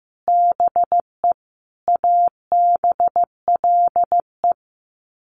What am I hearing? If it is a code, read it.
Morse code, heard as a keyed tone, about 15 wpm, 700 Hz. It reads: BE ABLE
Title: Sets of 5 Words 15wpm